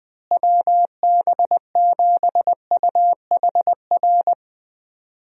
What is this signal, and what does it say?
Morse code, heard as a keyed tone, about 21 wpm, 700 Hz. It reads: WB7UHR
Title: USA Call Signs - 6 Characters - Repeated 3x - 20wpm